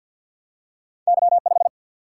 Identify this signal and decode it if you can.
Morse code, heard as a keyed tone, about 48 wpm, 700 Hz. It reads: Y5
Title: Single-Letter Single-Number 50wpm